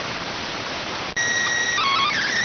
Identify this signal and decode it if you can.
SSTV (slow-scan television), PD 290.